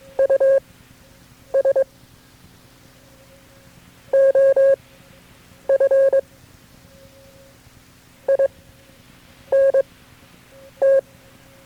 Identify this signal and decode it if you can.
Morse code, heard as a keyed tone, about 22 wpm, spaced as for 8 wpm, 540 Hz. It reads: US OF INT
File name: KB6MT-tape3-side1